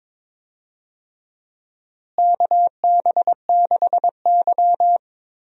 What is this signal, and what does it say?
Morse code, heard as a keyed tone, about 22 wpm, 700 Hz. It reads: KB6Y
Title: USA Call Signs - 4 Characters - Repeated 3x - 22wpm